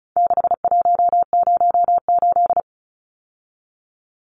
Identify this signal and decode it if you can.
Morse code, heard as a keyed tone, about 34 wpm, 700 Hz. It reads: 6108
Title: Numbers - 4-Digit 35wpm